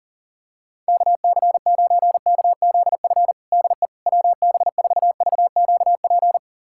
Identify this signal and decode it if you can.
Morse code, heard as a keyed tone, about 40 wpm, 700 Hz. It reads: KC9KZF DE WB4VQP
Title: USA Callsign DE Callsign - 6 Characters - 40wpm